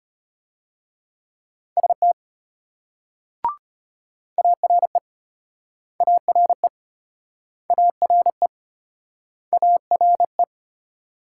Morse code, heard as a keyed tone, about 32 wpm, 700 Hz. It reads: ST ARE ARE ARE ARE